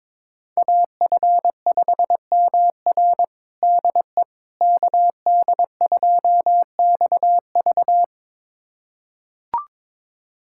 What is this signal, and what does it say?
Morse code, heard as a keyed tone, about 22 wpm, 700 Hz. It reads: AF5MR DE KD2XV
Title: USA Callsign DE Callsign - 5 Characters - 22wpm